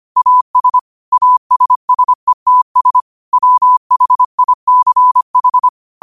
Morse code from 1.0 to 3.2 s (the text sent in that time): ASSETS